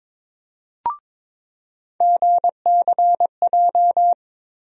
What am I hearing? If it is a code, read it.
Morse code, heard as a keyed tone, about 22 wpm, 700 Hz. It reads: GCJ
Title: USA Call Sign Postfixes - 3 Letters 22wpm